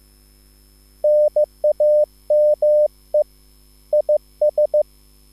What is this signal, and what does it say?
Morse code, heard as a keyed tone, about 15 wpm, 600 Hz. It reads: NAME IS
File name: QSL / CW lesson